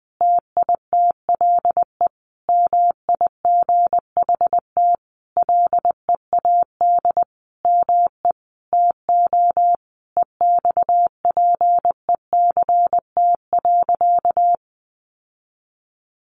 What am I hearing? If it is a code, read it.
Morse code, heard as a keyed tone, about 20 wpm, 700 Hz. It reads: TITLE MIGHT LEAD ME TO EXPECT.
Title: Sentences from Top 1200 Words 20wpm